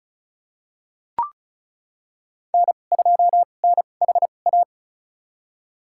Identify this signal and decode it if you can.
Morse code, heard as a keyed tone, about 34 wpm, spaced as for 25 wpm, 700 Hz. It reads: N2NHA
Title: USA Call Signs - 2x Character Spacing - 35wpm